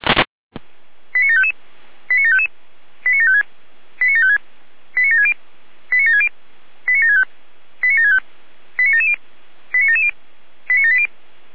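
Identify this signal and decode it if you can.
Five-tone selective call (ZVEI), heard as quick runs of five tones, tones of about 70 ms. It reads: 89755 89755 89756 89756 89760 89760 89765 89765 89770 89770 89780
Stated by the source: ZVEI five-tone selective call received on an SDR.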